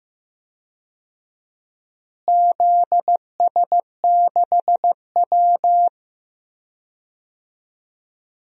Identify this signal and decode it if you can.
Morse code, heard as a keyed tone, about 15 wpm, 700 Hz. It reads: ZS6W
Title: DXCC-100 Call Signs - 4 Characters - Repeated 3x - 15wpm